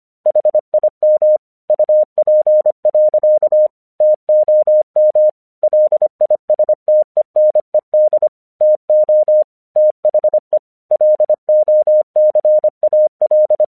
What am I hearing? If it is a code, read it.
Morse code, heard as a keyed tone, about 26 wpm, 610 Hz. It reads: HIM UP. TOM LISTENED TO THE LOCAL
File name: 1 English 25wpm / AQC14